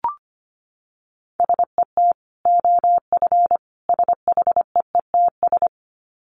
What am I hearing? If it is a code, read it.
Morse code, heard as a keyed tone, about 25 wpm, 700 Hz. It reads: SET OF SHEETS
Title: Dit Challenge 25wpm